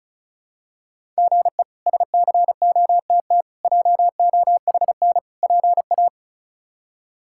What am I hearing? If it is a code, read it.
Morse code, heard as a keyed tone, about 36 wpm, 700 Hz. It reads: GE SCOTT JOHN PA
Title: K1USN SST QSO Practice - 35wpm